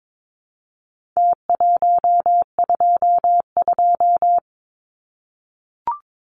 Morse code, heard as a keyed tone, about 22 wpm, 700 Hz. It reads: T122
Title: Numbers - Contest Style - 4-Digit 22wpm